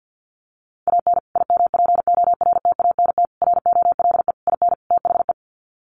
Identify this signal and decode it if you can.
Morse code, heard as a keyed tone, about 50 wpm, 700 Hz. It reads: AN IMPORTANT ROLE IN THE
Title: Sets of 5 Words Encore2 50wpm